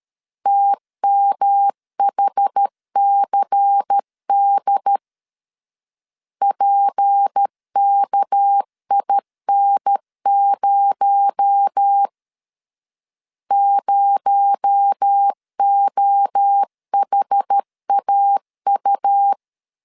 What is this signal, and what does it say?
Morse code, heard as a keyed tone, about 13 wpm, 800 Hz. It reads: TMHCD PKIN0 0OHAU
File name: LEZIONI / 6 PKX0 / 6 PKX0 12wpm 2231